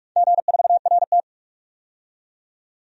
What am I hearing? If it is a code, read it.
Morse code, heard as a keyed tone, about 47 wpm, 700 Hz. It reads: G4RT
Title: DXCC-100 Call Signs - 4 Characters - 45wpm